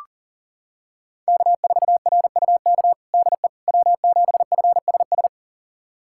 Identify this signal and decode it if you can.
Morse code, heard as a keyed tone, about 40 wpm, 700 Hz. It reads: K4RUK DE W7FSS
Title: USA Callsign DE Callsign - 5 Characters - 40wpm